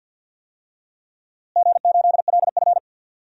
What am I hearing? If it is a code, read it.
Morse code, heard as a keyed tone, about 49 wpm, 700 Hz. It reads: G8LF
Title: DXCC-100 Call Signs - 4 Characters - Repeated 3x - 50wpm